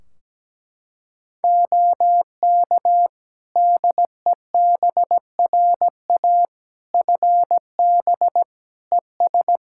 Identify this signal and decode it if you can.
Morse code, heard as a keyed tone, about 17 wpm, 700 Hz. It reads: OK DEBRA FB ES
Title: Practice QSO - Exchange 2 - Info - 17wpm